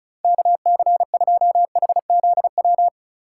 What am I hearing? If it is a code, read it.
Morse code, heard as a keyed tone, about 36 wpm, 700 Hz. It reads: KC2HZW
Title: USA Call Signs - 6 Characters - 4x Character Spacing - 35wpm